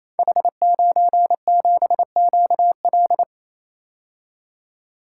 Morse code, heard as a keyed tone, about 27 wpm, 700 Hz. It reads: H97QL
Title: Code Groups - 5 Letter:Numbers - 28wpm